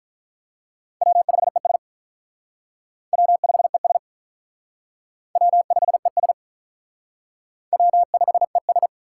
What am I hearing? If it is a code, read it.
Morse code, heard as a keyed tone, about 45 wpm, 700 Hz. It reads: W5ES W5ES W5ES W5ES